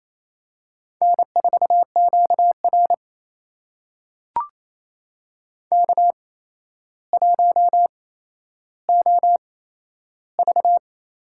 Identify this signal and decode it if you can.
Morse code, heard as a keyed tone, about 28 wpm, spaced as for 12 wpm, 700 Hz. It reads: N4QR K1OV
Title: USA Call Signs - 4 Characters - 8x Character Spacing - 28wpm